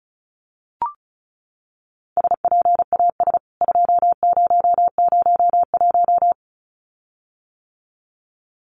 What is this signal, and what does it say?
Morse code, heard as a keyed tone, about 34 wpm, 700 Hz. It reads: SPAS 2001
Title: CWT Exchange 35wpm